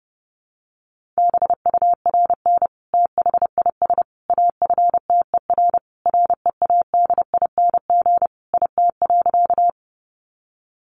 Morse code, heard as a keyed tone, about 28 wpm, 700 Hz. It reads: BURN THIS AFTER READING IT.
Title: Sentences from Top 1300 Words 30wpm